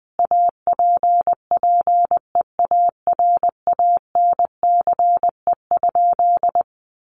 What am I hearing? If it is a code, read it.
Morse code, heard as a keyed tone, about 20 wpm, 700 Hz. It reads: APPEARANCE?
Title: Sentences from Top 1100 Words 20wpm